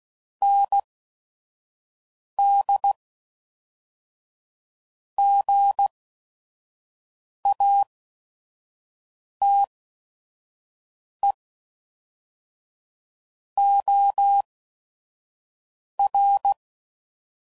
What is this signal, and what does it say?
Morse code, heard as a keyed tone, about 16 wpm, spaced as for 5 wpm, 800 Hz. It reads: ND GATE OR
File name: CW Code Groups @5WPM / Technical Subjects / AND Gate